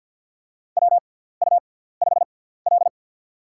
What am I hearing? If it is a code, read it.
Morse code, heard as a keyed tone, about 53 wpm, spaced as for 17 wpm, 700 Hz. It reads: WU5L
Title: USA Call Signs - 4 Characters - 6x Character Spacing - 50wpm